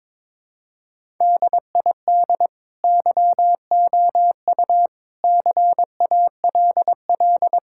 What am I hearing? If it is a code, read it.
Morse code, heard as a keyed tone, about 21 wpm, 700 Hz. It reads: DID YOU CALL
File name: practice / 22wpm / Sentences with 2-Word Phrases - 22wpm